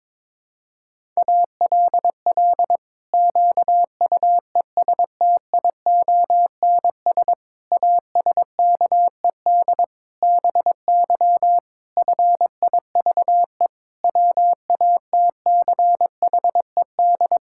Morse code, heard as a keyed tone, about 22 wpm, 700 Hz. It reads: ALL QUESTIONS ASKED BY FIVE WATCHED
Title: Pangram Challenge v2 - 22wpm